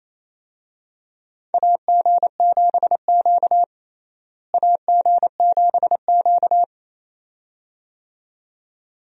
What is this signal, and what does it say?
Morse code, heard as a keyed tone, about 28 wpm, 700 Hz. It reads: AG7Q AG7Q
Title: USA Call Signs - 4 Characters - Repeated 2x - 28wpm